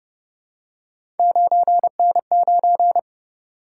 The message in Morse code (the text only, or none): 9N9